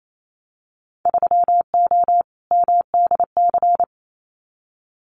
Morse code, heard as a keyed tone, about 28 wpm, 690 Hz. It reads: 3O MDC
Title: Winter Field Day Exchanges Encore 28wpm